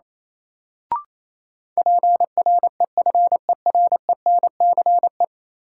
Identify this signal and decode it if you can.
Morse code, heard as a keyed tone, about 28 wpm, 700 Hz. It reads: PREFERENCE
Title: Sentences from Top 1500 Words - New Word Drill Alt - 28wpm